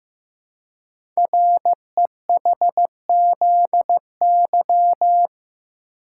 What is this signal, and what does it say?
Morse code, heard as a keyed tone, about 15 wpm, 700 Hz. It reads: REHZY
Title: Code Groups - 5 Letters - 15wpm